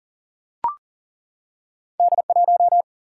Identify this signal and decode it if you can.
Morse code, heard as a keyed tone, about 41 wpm, 700 Hz. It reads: D1
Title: Single-Letter Single-Number 40wpm